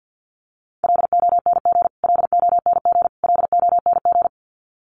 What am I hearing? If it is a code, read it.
Morse code, heard as a keyed tone, about 50 wpm, 700 Hz. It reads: LONG LONG LONG
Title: Top 300 Words Repeated 50wpm